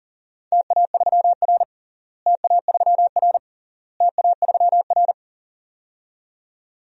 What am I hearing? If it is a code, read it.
Morse code, heard as a keyed tone, about 40 wpm, 700 Hz. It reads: TA3R TA3R TA3R
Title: DXCC-100 Call Signs - 4 Characters - Repeated 3x - 40wpm